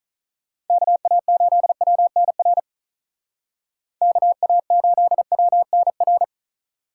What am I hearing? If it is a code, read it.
Morse code, heard as a keyed tone, about 45 wpm, 700 Hz. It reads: KA8WNR KA8WNR